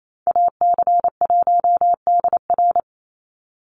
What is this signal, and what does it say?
Morse code, heard as a keyed tone, about 28 wpm, 700 Hz. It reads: AC1DR
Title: USA Call Signs 28wpm